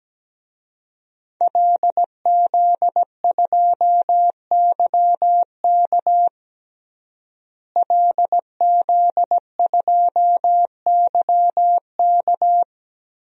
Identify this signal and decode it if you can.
Morse code, heard as a keyed tone, about 17 wpm, 700 Hz. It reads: LZ2YK LZ2YK
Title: DXCC-100 Call Signs - 5 Characters - Repeated 2x - 17wpm